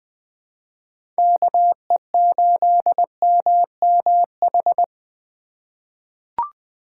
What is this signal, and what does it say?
Morse code, heard as a keyed tone, about 20 wpm, 700 Hz. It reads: KE8MMH
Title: USA Call Signs - 6 Characters - 4x Character Spacing - 20wpm